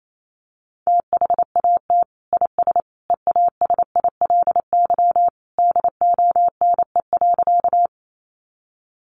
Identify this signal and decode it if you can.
Morse code, heard as a keyed tone, about 28 wpm, 700 Hz. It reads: THAT IS EASILY DONE.
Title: Sentences from Top 900 Words 28wpm